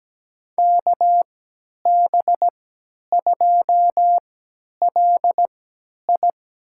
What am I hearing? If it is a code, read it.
Morse code, heard as a keyed tone, about 17 wpm, spaced as for 10 wpm, 700 Hz. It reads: KB2LI
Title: USA Call Signs - 3x Character Spacing - 17wpm